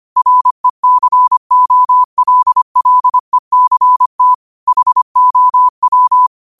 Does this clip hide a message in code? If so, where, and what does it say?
Yes: Morse code from 0.2 to 6.3 s, heard as a keyed tone, about 25 wpm, 1000 Hz. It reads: RECOLLECT HOW